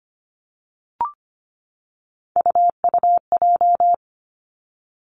Morse code, heard as a keyed tone, about 25 wpm, 700 Hz. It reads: UUJ
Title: Three Challenging Letter-Number Combinations 25wpm